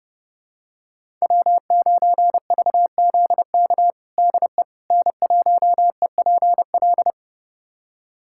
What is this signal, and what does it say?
Morse code, heard as a keyed tone, about 30 wpm, 700 Hz. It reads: W9VZK DE N1EPL
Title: USA Callsign DE Callsign - 5 Characters - 30wpm